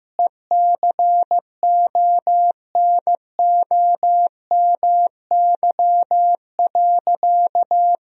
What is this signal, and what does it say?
Morse code, heard as a keyed tone, about 14 wpm, 700 Hz. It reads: ECONOMY.
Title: Sentences from Top 1000 Words - Review Part 3 - 15wpm